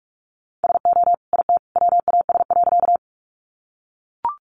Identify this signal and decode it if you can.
Morse code, heard as a keyed tone, about 44 wpm, 700 Hz. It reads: SO IT WAS.
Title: Sentences from Top 100 Words 45wpm